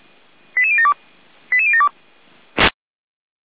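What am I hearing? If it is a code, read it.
Five-tone selective call (ZVEI), heard as quick runs of five tones, tones of about 70 ms. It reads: 88072 88072
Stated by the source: Received by SDR; ZVEI five-tone code 88125.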